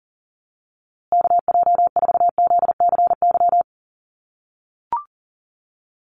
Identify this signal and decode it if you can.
Morse code, heard as a keyed tone, about 39 wpm, 700 Hz. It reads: KJ4ZCY
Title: USA Call Signs Repeated 2x 40wpm